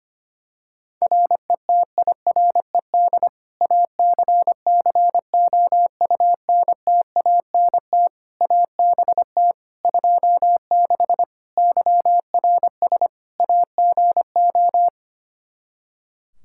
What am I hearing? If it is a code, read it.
Morse code, heard as a keyed tone, about 25 wpm, 700 Hz. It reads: RETIRED ACCOUNTANT ABT 26 YRS AGO
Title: Practice QSO - Exchange 3 - More Info - 25wpm